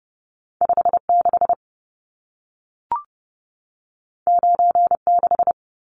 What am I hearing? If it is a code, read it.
Morse code, heard as a keyed tone, about 30 wpm, 700 Hz. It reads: 56 96
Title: Numbers - 2-Digit 30wpm